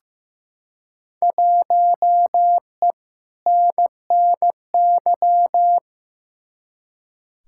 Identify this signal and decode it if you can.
Morse code, heard as a keyed tone, about 15 wpm, 700 Hz. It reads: 1E NNY
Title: Field Day Exchanges - 15wpm